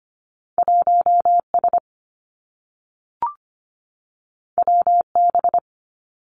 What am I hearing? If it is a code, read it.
Morse code, heard as a keyed tone, about 25 wpm, 700 Hz. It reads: 1 S W B
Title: Two Challenging Letter-Number Combinations 25wpm